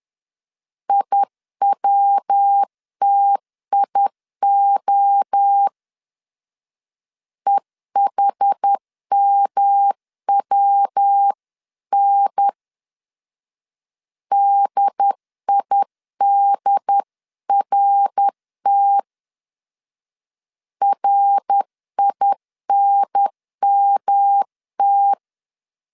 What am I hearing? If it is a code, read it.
Morse code, heard as a keyed tone, about 10 wpm, 800 Hz. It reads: IWTIO EHMWN DIDRT RINMT
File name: 10wpm